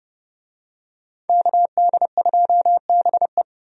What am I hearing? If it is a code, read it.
Morse code, heard as a keyed tone, about 30 wpm, 700 Hz. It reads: KD2BE
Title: USA Call Signs - 5 Characters - 30wpm